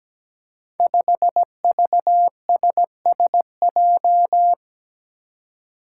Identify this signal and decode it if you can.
Morse code, heard as a keyed tone, about 17 wpm, 700 Hz. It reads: 5VSSJ
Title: Five Challenging Letter-Number Combinations 17wpm